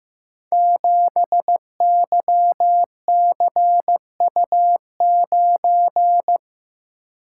Morse code, heard as a keyed tone, about 15 wpm, 700 Hz. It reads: 7YCU9
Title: Code Groups - 5 Letter:Numbers - 15wpm